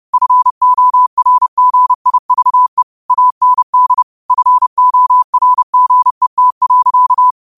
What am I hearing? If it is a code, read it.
Morse code, heard as a keyed tone, about 30 wpm, 1000 Hz. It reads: FORGIVE AND FORGET.
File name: qotdfiles / 30wpm